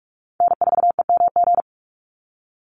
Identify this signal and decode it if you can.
Morse code, heard as a keyed tone, about 44 wpm, 700 Hz. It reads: N4EMG
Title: NA SOTA Chaser Callsigns 45wpm